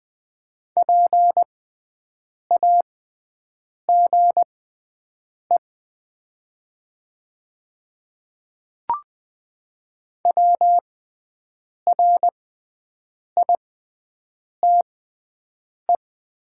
Morse code, heard as a keyed tone, about 20 wpm, spaced as for 7 wpm, 700 Hz. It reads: PAGE WRITE
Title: 6x Character Spacing - Top 300 Words - 20wpm